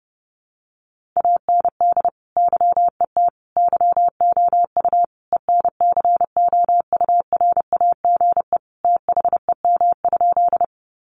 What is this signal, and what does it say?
Morse code, heard as a keyed tone, about 30 wpm, 700 Hz. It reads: AND YET YOU ENCOURAGE THEM?
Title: Sentences from Top 1400 Words 30wpm